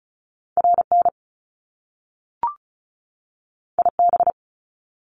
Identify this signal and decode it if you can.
Morse code, heard as a keyed tone, about 36 wpm, 700 Hz. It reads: RN IB
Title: Top Two Letter Combinations in Words 35wpm